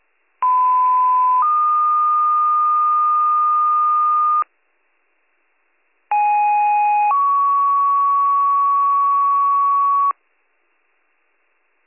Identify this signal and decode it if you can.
Two-tone page, heard as a one-second tone, then a three-second tone, about 1010 then 1220 Hz, about 830 then 1120 Hz. It is heard from 0.4 to 10.1 s.